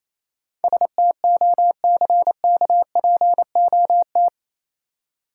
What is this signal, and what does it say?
Morse code, heard as a keyed tone, about 27 wpm, 700 Hz. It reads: STOCKPOT
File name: Easy Words - Speed-Racing with Hints - 22wpm